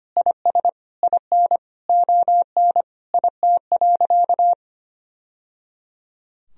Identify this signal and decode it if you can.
Morse code, heard as a keyed tone, about 25 wpm, 700 Hz. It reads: IS IN ON IT.
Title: Sentences from Top 1000 Words - Review Part 4 - 25wpm